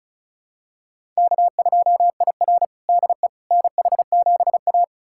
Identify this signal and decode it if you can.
Morse code, heard as a keyed tone, about 36 wpm, 700 Hz. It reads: K2IR DE NH7A
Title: USA Callsign DE Callsign - 4 Characters - 35wpm